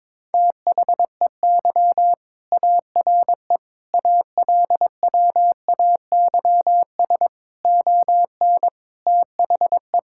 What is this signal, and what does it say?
Morse code, heard as a keyed tone, about 22 wpm, 700 Hz. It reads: THEY ARE ALWAYS ON THE
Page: morse-code-ninja-practice-22wpm